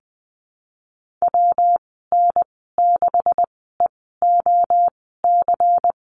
Morse code, heard as a keyed tone, about 20 wpm, spaced as for 14 wpm, 700 Hz. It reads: WN6EOC
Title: 147 Lesson 24g - Letter y - Call Signs 20wpm